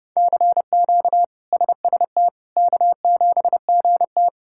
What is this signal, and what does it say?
Morse code, heard as a keyed tone, about 30 wpm, 700 Hz. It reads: CQ SST K7GT
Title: K1USN SST QSO Practice - 30wpm